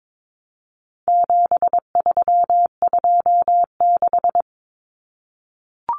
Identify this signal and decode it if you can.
Morse code, heard as a keyed tone, about 22 wpm, 700 Hz. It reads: 7326
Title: Common Cut Numbers - 4-Digit 22wpm